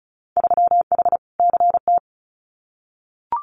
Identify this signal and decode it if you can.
Morse code, heard as a keyed tone, about 34 wpm, 700 Hz. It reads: 3H CT
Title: Winter Field Day Exchanges Encore 35wpm